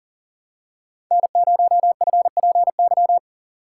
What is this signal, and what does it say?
Morse code, heard as a keyed tone, about 40 wpm, 700 Hz. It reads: N0FPY
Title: USA Call Signs - 5 Characters - 4x Character Spacing - 40wpm